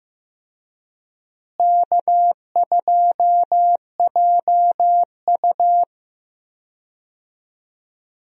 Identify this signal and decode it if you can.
Morse code, heard as a keyed tone, about 15 wpm, 700 Hz. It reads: K2JU
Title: USA Call Signs - 4 Characters - Repeated 2x - 15wpm